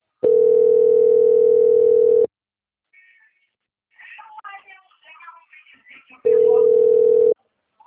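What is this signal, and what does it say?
Ringback tone (North America), 440 + 480 Hz.
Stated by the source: Outgoing telephone call recorded on the Atibaia 1 CP Cartao system.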